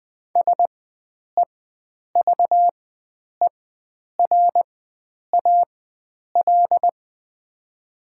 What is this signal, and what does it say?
Morse code, heard as a keyed tone, about 20 wpm, spaced as for 9 wpm, 700 Hz. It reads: SEVERAL
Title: 4x Character Spacing - Top 500 Words - 20wpm